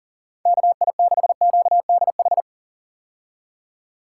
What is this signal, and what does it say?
Morse code, heard as a keyed tone, about 40 wpm, 700 Hz. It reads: KI6QDH
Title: K1USN SST QSO Practice - 40wpm